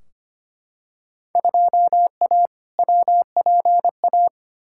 Morse code, heard as a keyed tone, about 25 wpm, 700 Hz. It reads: 2A WPA
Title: Field Day Exchanges - 25wpm